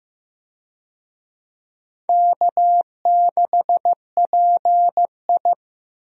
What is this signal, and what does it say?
Morse code, heard as a keyed tone, about 15 wpm, 700 Hz. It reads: K6PI
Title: USA Call Signs - 4 Characters - Repeated 2x - 15wpm